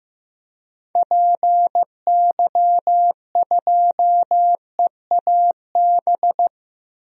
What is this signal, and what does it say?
Morse code, heard as a keyed tone, about 15 wpm, 700 Hz. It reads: PY2EAB
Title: DXCC-100 Call Signs - 6 Characters - 15wpm